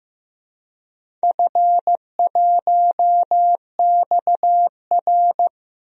Morse code, heard as a keyed tone, about 15 wpm, 700 Hz. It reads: F1XR
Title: DXCC-100 Call Signs - 4 Characters - 15wpm